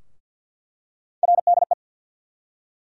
Morse code, heard as a keyed tone, about 47 wpm, 700 Hz. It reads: RDE
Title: USA Call Sign Postfixes - 3 Letters 50wpm